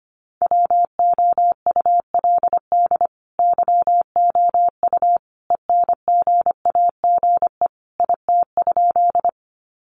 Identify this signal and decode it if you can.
Morse code, heard as a keyed tone, about 25 wpm, 700 Hz. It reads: WOULD YOU ENGAGE IT?
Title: Sentences from Top 1500 Words 25wpm